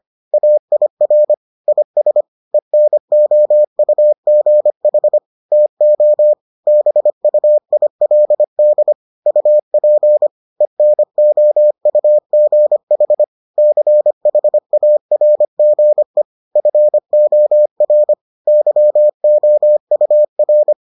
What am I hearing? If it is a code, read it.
Morse code, heard as a keyed tone, about 25 wpm, 600 Hz. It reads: AIR IS ENOUGH TO BUILD UP ENOUGH CHARGE FOR YOUR